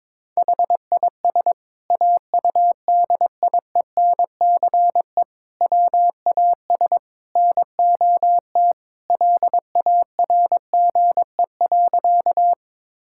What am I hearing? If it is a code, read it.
Morse code, heard as a keyed tone, about 21 wpm, 700 Hz. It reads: HIS AUDIENCE WAS NOT LARGE.
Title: Sentences from Top 1000 Words - Review Part 5 - 22wpm